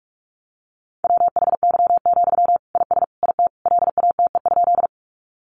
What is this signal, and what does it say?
Morse code, heard as a keyed tone, about 44 wpm, 700 Hz. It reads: WHY, IS IT LATE?
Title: Sentences from Top 300 Words 45wpm